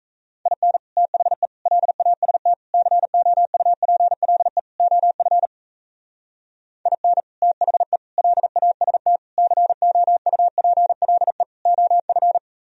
Morse code, heard as a keyed tone, about 34 wpm, 700 Hz. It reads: IN THE LAST COUPLE OF IN THE LAST COUPLE OF
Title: Speed-Racing - Sets of 5 Words 28wpm